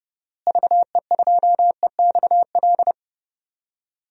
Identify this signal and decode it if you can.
Morse code, heard as a keyed tone, about 30 wpm, 700 Hz. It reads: VE2EXL
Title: DXCC-100 Call Signs - 6 Characters - Repeated 3x - 30wpm